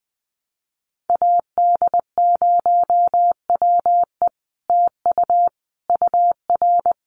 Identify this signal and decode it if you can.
Morse code, heard as a keyed tone, about 20 wpm, 700 Hz. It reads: AD0WE TU UR
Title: Common QSO Phrases Encore 20wpm